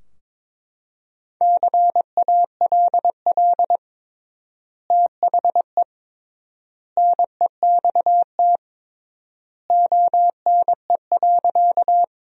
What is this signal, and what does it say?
Morse code, heard as a keyed tone, about 21 wpm, 700 Hz. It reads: CALL THE NEXT ONE.
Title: 3x Word-Spacing - Sentences from Top 300 Words - 22wpm